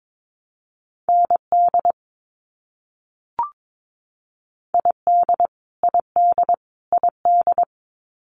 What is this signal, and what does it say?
Morse code, heard as a keyed tone, about 22 wpm, 700 Hz. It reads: ND ID ID ID
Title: US States Repeated 3x 22wpm